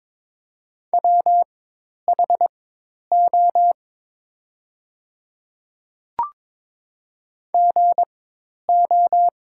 Morse code, heard as a keyed tone, about 22 wpm, spaced as for 10 wpm, 700 Hz. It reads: WHO GO
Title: 4x Character Spacing - Top 100 Words - 22wpm